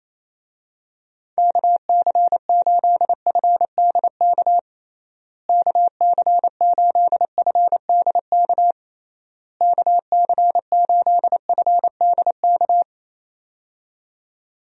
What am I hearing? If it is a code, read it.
Morse code, heard as a keyed tone, about 27 wpm, 700 Hz. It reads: KC8FDK KC8FDK KC8FDK
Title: USA Call Signs - 6 Characters - Repeated 3x - 28wpm